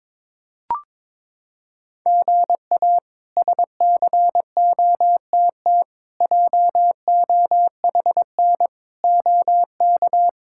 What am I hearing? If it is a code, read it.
Morse code, heard as a keyed tone, about 22 wpm, 700 Hz. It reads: GA SCOTT JOHN OK
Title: K1USN SST QSO Practice - 22wpm